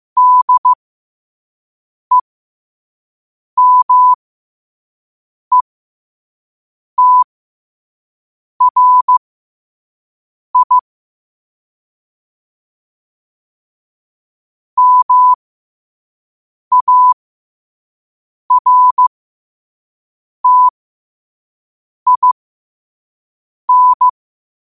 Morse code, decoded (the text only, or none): DEMETRI MARTIN